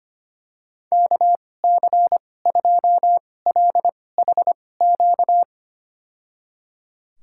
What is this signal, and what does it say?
Morse code, heard as a keyed tone, about 25 wpm, spaced as for 18 wpm, 700 Hz. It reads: KC2LHQ
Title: USA Call Signs - 6 Characters - 2x Character Spacing - 25wpm